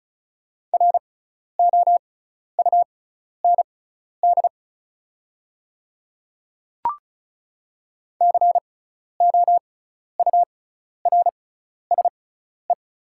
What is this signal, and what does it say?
Morse code, heard as a keyed tone, about 34 wpm, spaced as for 12 wpm, 700 Hz. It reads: ROUND COURSE
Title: 6x Character Spacing - Top 500 Words - 35wpm